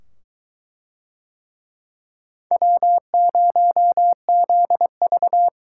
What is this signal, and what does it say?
Morse code, heard as a keyed tone, about 22 wpm, 700 Hz. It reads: W0ZV